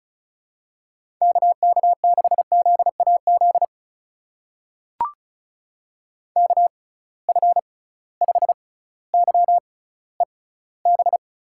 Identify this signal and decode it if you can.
Morse code, heard as a keyed tone, about 36 wpm, spaced as for 18 wpm, 700 Hz. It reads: KK6ZAZ KF5YEB
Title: USA Call Signs - 6 Characters - 6x Character Spacing - 35wpm